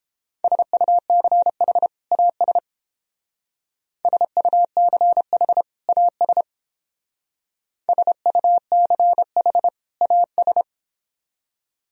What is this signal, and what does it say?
Morse code, heard as a keyed tone, about 30 wpm, 700 Hz. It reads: SUCH AS SUCH AS SUCH AS